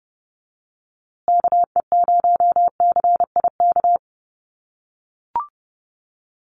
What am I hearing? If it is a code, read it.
Morse code, heard as a keyed tone, about 30 wpm, 700 Hz. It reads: KE0CIK
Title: USA Call Signs Repeated 3x 30wpm